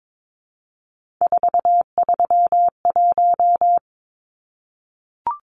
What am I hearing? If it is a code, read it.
Morse code, heard as a keyed tone, about 22 wpm, 700 Hz. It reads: 431
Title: Numbers - 3-Digit 22wpm